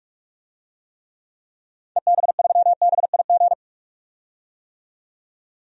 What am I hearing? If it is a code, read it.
Morse code, heard as a keyed tone, about 43 wpm, 700 Hz. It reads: EB3BIG
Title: DXCC-100 Call Signs - 6 Characters - 45wpm